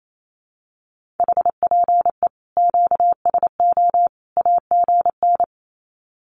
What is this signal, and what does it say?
Morse code, heard as a keyed tone, about 28 wpm, 700 Hz. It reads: HPE QSO AGN
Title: Common QSO Phrases 28wpm